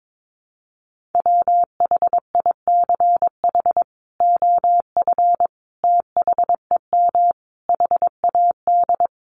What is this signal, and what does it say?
Morse code, heard as a keyed tone, about 22 wpm, 700 Hz. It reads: WHICH OF THEM HAD
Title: Sentences from Top 1100 Words 22wpm